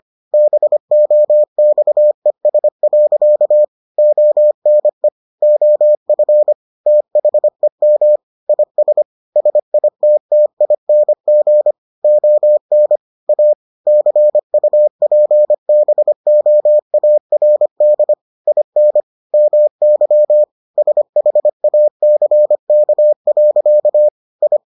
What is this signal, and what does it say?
Morse code, heard as a keyed tone, about 25 wpm, 600 Hz. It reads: BOXES. ONE OF THEM IS SITTING ON A CUPBOARD IN MY SHACK. I